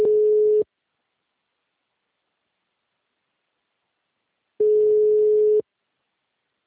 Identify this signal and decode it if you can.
Ringback tone (Europe, 425 Hz).
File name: out- -2024- - - .585